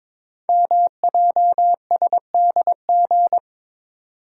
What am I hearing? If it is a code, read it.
Morse code, heard as a keyed tone, about 22 wpm, 700 Hz. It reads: MJSDG
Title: Code Groups - 5 Letters - 22wpm